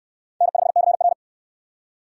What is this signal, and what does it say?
Morse code, heard as a keyed tone, about 63 wpm, 700 Hz. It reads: N5CR